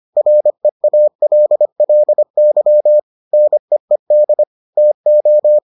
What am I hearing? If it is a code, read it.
Morse code, heard as a keyed tone, about 25 wpm, 600 Hz. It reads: REALLY NEED TO